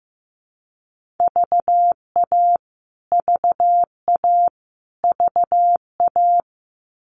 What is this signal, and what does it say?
Morse code, heard as a keyed tone, about 15 wpm, 700 Hz. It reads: VA VA VA
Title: US States Repeated 3x 15wpm